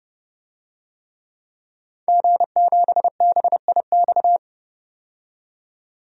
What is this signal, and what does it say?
Morse code, heard as a keyed tone, about 30 wpm, 700 Hz. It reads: G7BIX
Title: DXCC-100 Call Signs - 5 Characters - 30wpm